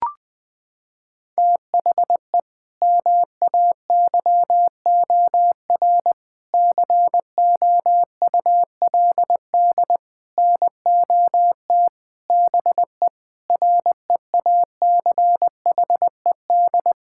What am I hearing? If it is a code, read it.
Morse code, heard as a keyed tone, about 20 wpm, 700 Hz. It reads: THE MAYOR COULD NOT BE REACHED
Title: Sentences with 6-Word Phrases - 20wpm